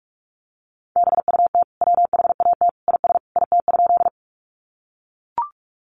Morse code, heard as a keyed tone, about 44 wpm, 700 Hz. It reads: BUT WHAT IS IT?
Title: Sentences from Top 100 Words 45wpm